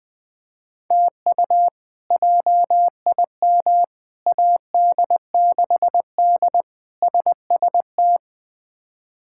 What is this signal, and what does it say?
Morse code, heard as a keyed tone, about 20 wpm, 700 Hz. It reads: TU JIM AD6D SST
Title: K1USN SST QSO Practice - 20wpm